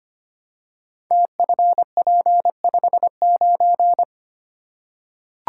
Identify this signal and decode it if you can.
Morse code, heard as a keyed tone, about 25 wpm, 700 Hz. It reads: TFP59
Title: Code Groups - 5 Letter:Numbers - 25wpm